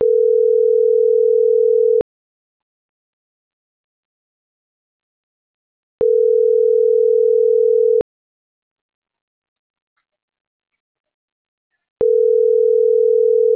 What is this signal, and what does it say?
Ringback tone (North America), 440 + 480 Hz.